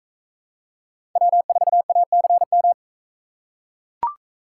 Morse code, heard as a keyed tone, about 40 wpm, 700 Hz. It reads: W4ACM